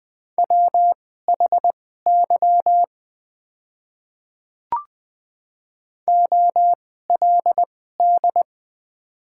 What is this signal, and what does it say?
Morse code, heard as a keyed tone, about 20 wpm, spaced as for 14 wpm, 700 Hz. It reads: WHY OLD
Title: 2x Character Spacing - Top 500 Words - 20wpm